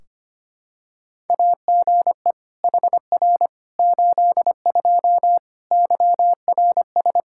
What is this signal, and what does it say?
Morse code, heard as a keyed tone, about 25 wpm, 700 Hz. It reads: AGE HR 82 YRS
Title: K1USN SST QSO Practice 25wpm